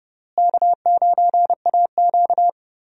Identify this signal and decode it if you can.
Morse code, heard as a keyed tone, about 30 wpm, 700 Hz. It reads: K9AQ
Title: USA Call Signs - 4 Characters - Repeated 2x - 30wpm